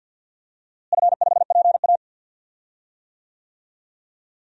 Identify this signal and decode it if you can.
Morse code, heard as a keyed tone, about 52 wpm, 700 Hz. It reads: F5PA
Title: DXCC-100 Call Signs - 4 Characters - Repeated 2x - 50wpm